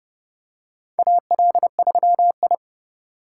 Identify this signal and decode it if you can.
Morse code, heard as a keyed tone, about 30 wpm, 700 Hz. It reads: AL3I
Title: USA Call Signs - 4 Characters - 2x Character Spacing - 30wpm